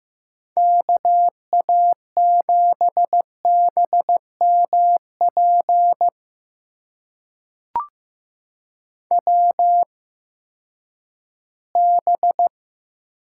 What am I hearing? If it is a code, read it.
Morse code, heard as a keyed tone, about 15 wpm, spaced as for 10 wpm, 700 Hz. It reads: KA7BMP WB
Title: USA Call Signs - 6 Characters - 8x Character Spacing - 15wpm